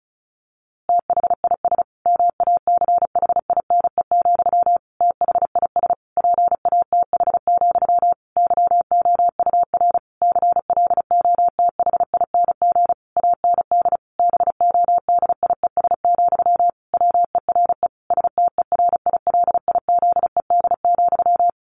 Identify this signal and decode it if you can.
Morse code, heard as a keyed tone, about 34 wpm, 700 Hz. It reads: THIS MACHINE, THIS PATH, YOUR CLOTHING AND BODIES, WERE STERILIZED,
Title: A Sound Of Thunder 35wpm